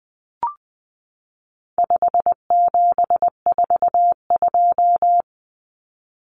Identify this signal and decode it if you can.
Morse code, heard as a keyed tone, about 20 wpm, 700 Hz. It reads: 5742
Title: Common Cut Numbers - 4-Digit 20wpm